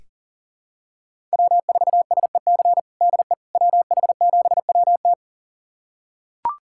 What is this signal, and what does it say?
Morse code, heard as a keyed tone, about 40 wpm, 700 Hz. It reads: W4SEC DE WH7WT
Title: USA Callsign DE Callsign - 5 Characters - 40wpm